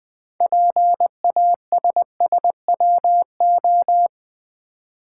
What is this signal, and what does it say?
Morse code, heard as a keyed tone, about 20 wpm, 700 Hz. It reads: PASSWO
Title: Build A Word - 2 Letters At A Time - 8-Letter Words - 20wpm